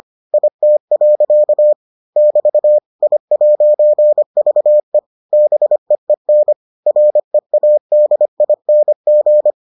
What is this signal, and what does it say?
Morse code, heard as a keyed tone, about 25 wpm, 600 Hz. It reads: IT. <BT> I'VE BEEN READING